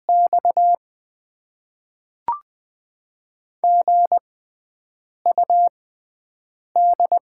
Morse code, heard as a keyed tone, about 20 wpm, spaced as for 7 wpm, 690 Hz. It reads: X GUD
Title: Top 20 QSO Elements - 6x Character Spacing - 20wpm